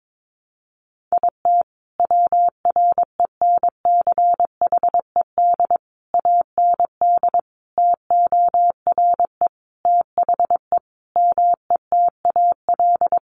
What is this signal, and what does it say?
Morse code, heard as a keyed tone, about 22 wpm, 700 Hz. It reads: IT WRENCHED AND TORE THE METAL
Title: A Sound Of Thunder 22wpm